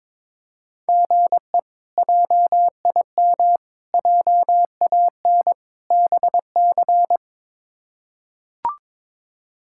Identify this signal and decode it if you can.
Morse code, heard as a keyed tone, about 22 wpm, 700 Hz. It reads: GE JIM JAN BC
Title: K1USN SST QSO Practice - 22wpm